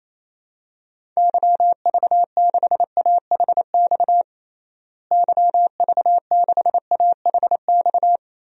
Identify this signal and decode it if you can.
Morse code, heard as a keyed tone, about 28 wpm, 700 Hz. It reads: YV6AHX YV6AHX
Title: DXCC-100 Call Signs - 6 Characters - Repeated 3x - 28wpm